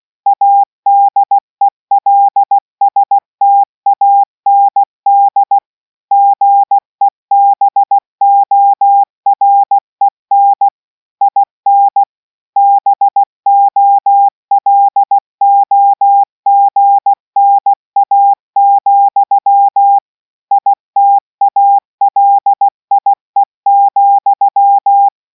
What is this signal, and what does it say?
Morse code, heard as a keyed tone, about 16 wpm, 800 Hz. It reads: ADELSTAND GEBOREN IN BOLOGNA, ITALIE,